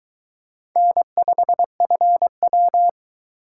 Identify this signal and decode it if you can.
Morse code, heard as a keyed tone, about 20 wpm, 700 Hz. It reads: N5FW